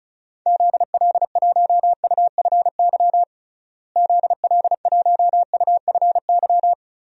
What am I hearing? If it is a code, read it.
Morse code, heard as a keyed tone, about 36 wpm, 700 Hz. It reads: ZL1UFY ZL1UFY
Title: DXCC-100 Call Signs - 6 Characters - Repeated 2x - 35wpm